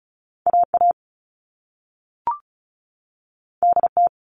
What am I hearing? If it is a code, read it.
Morse code, heard as a keyed tone, about 36 wpm, 690 Hz. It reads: AA DT
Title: All Cut Numbers - 2-Digit 35wpm